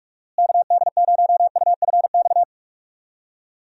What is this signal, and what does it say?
Morse code, heard as a keyed tone, about 46 wpm, 700 Hz. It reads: KD0UFX
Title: USA Call Signs - 6 Characters - Repeated 2x - 45wpm